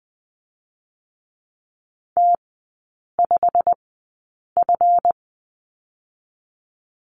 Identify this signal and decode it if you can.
Morse code, heard as a keyed tone, about 20 wpm, spaced as for 8 wpm, 700 Hz. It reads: T5F
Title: 205 Lesson 33e - Letter j - Three Characters 20wpm